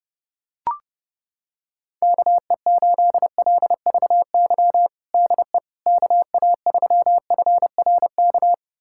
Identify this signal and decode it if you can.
Morse code, heard as a keyed tone, about 30 wpm, 700 Hz. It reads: KE8LVY DE KA3FRK
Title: USA Callsign DE Callsign - 6 Characters - 30wpm